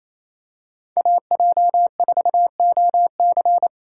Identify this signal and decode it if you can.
Morse code, heard as a keyed tone, about 28 wpm, 700 Hz. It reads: AJ4OC
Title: USA Call Signs - 5 Characters - 4x Character Spacing - 28wpm